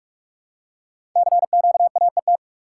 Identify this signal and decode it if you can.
Morse code, heard as a keyed tone, about 44 wpm, 700 Hz. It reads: CQAET
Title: Code Groups - 5 Letter:Numbers - 45wpm